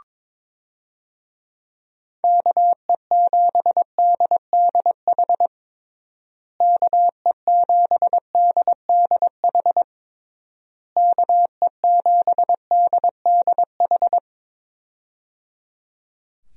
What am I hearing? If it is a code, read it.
Morse code, heard as a keyed tone, about 22 wpm, 700 Hz. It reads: KE7DDH KE7DDH KE7DDH
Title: USA Call Signs - 6 Characters - Repeated 3x - 22wpm